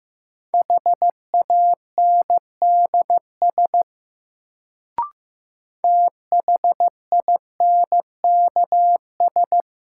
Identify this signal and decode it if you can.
Morse code, heard as a keyed tone, about 15 wpm, 700 Hz. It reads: HANDS THINKS
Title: Sentences from Top 1000 Words - Review Part 6 - New Word Drill - 15wpm